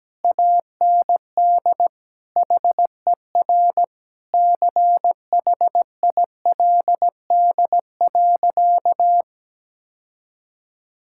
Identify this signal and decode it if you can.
Morse code, heard as a keyed tone, about 17 wpm, 700 Hz. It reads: AND HER CHILD.
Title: Sentences from Top 1600 Words 17wpm